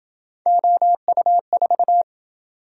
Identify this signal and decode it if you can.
Morse code, heard as a keyed tone, about 27 wpm, 700 Hz. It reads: OU4